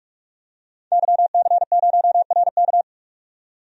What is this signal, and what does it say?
Morse code, heard as a keyed tone, about 46 wpm, 700 Hz. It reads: YC0RK
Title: DXCC-100 Call Signs - 5 Characters - 45wpm